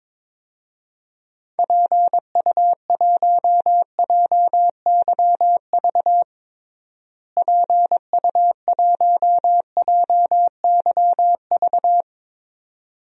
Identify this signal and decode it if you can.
Morse code, heard as a keyed tone, about 22 wpm, 700 Hz. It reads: PU1JYV PU1JYV
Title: DXCC-100 Call Signs - 6 Characters - Repeated 2x - 22wpm